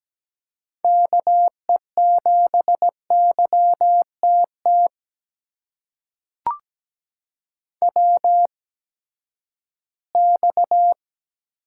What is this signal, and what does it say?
Morse code, heard as a keyed tone, about 17 wpm, spaced as for 12 wpm, 700 Hz. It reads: KE7YTT WX
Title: USA Call Signs - 6 Characters - 8x Character Spacing - 17wpm